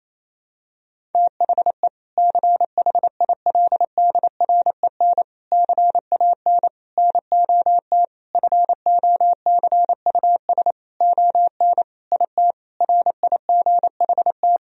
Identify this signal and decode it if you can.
Morse code, heard as a keyed tone, about 28 wpm, 700 Hz. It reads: THE CHILDREN CAN NOT FOCUS ON IT RIGHT
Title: Sentences from Top 1000 Words - Review Part 4 - 28wpm